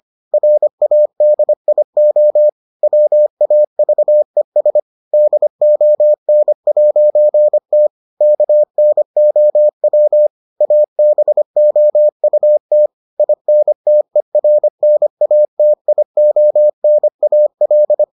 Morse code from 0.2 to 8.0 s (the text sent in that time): RADIO WAVES DON'T